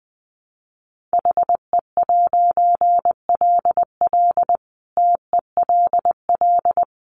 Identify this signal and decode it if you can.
Morse code, heard as a keyed tone, about 20 wpm, 700 Hz. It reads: HE'LL TELL
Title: A Sound Of Thunder 20wpm